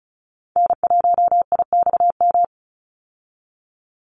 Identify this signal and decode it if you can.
Morse code, heard as a keyed tone, about 34 wpm, 700 Hz. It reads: N1IXM
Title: USA Call Signs 35wpm